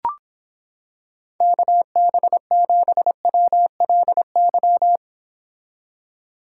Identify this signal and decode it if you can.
Morse code, heard as a keyed tone, about 26 wpm, 700 Hz. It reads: KB7WLY